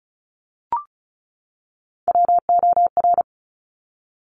Morse code, heard as a keyed tone, about 36 wpm, 700 Hz. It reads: WOR
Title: Top Three Letter Combinations in Words 35wpm